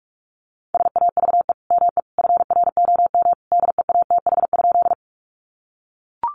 Morse code, heard as a keyed tone, about 44 wpm, 700 Hz. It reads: SAVE ME FROM DEATH?
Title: Sentences from Top 1100 Words 45wpm